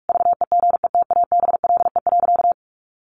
Morse code, heard as a keyed tone, about 44 wpm, 700 Hz. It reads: VEGETABLE.
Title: Sentences from Top 1300 Words 45wpm